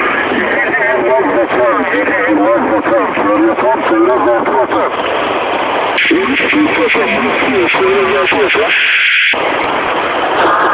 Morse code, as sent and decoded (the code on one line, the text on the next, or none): none